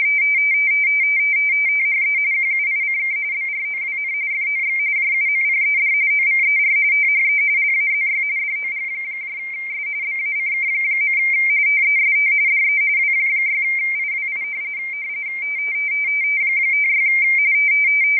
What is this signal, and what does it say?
RTTY, about 45.45 baud. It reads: THE QUICK BROWN FOX JUMPED OVER THE LAZY DOGS BACK. TEST DE W6IWI 56
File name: rtty / websdr recording 14086.0kHz